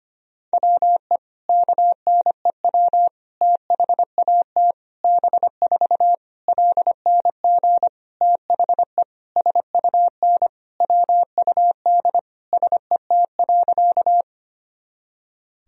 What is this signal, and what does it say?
Morse code, heard as a keyed tone, about 25 wpm, 700 Hz. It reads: WE KNEW THAT B4 LNG THE SUN WUD SET.
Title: Sentences from Top 400 Words and Abbreviations 25wpm